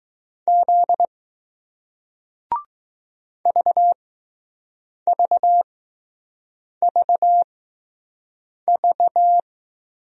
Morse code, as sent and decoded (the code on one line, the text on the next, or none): --.. / ...- ...- ...- ...-
Z VVVV